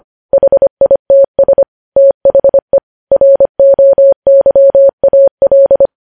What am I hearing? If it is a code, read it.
Morse code, heard as a keyed tone, about 25 wpm, 550 Hz. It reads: HITS THE ROYAL
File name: podcast / News headlines-1x-25x25